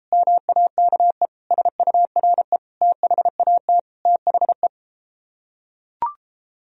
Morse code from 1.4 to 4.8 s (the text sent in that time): SURE THAT THE